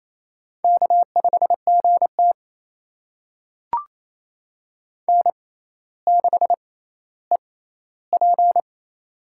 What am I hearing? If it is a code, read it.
Morse code, heard as a keyed tone, about 28 wpm, spaced as for 14 wpm, 700 Hz. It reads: K5GT N6EP
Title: USA Call Signs - 4 Characters - 6x Character Spacing - 28wpm